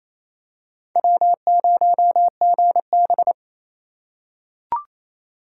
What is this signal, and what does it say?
Morse code, heard as a keyed tone, about 28 wpm, 700 Hz. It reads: W0GB
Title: USA Call Signs - 2x Character Spacing - 28wpm